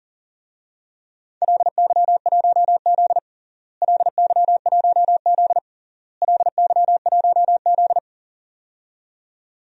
Morse code, heard as a keyed tone, about 40 wpm, 700 Hz. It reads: LY1Z LY1Z LY1Z
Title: DXCC-100 Call Signs - 4 Characters - Repeated 3x - 40wpm